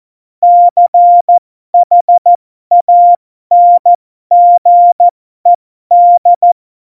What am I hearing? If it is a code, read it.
Morse code, heard as a keyed tone, about 14 wpm, spaced as for 12 wpm, 700 Hz. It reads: CHANGED
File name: Morsekurs / Witze